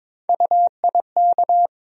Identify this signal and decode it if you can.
Morse code, heard as a keyed tone, about 21 wpm, 700 Hz. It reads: UIK
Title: USA Call Sign Postfixes - 3 Letters 22wpm